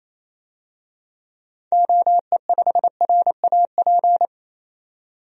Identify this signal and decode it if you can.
Morse code, heard as a keyed tone, about 28 wpm, 700 Hz. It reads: OE5RAP
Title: DXCC-100 Call Signs - 6 Characters - 28wpm